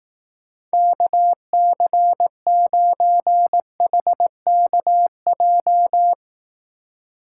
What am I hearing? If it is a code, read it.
Morse code, heard as a keyed tone, about 18 wpm, 700 Hz. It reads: KC9HKJ
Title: Speed Racing - USA Call Signs 15wpm